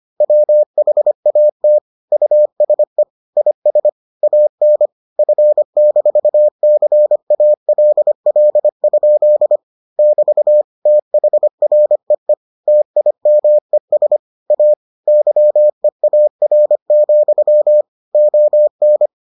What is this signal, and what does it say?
Morse code, heard as a keyed tone, about 25 wpm, 600 Hz. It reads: WHAT USE IS AN F-CALL? <BT> THREE TIMES A YEAR, ON